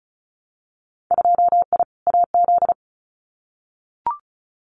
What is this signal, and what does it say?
Morse code, heard as a keyed tone, about 36 wpm, 700 Hz. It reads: 2I AZ
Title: Winter Field Day Exchanges Encore 35wpm